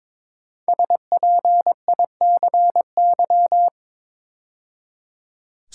Morse code, heard as a keyed tone, about 22 wpm, 700 Hz. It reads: SPICY
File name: 5 Letter Words - First 3 Letters Given - 22wpm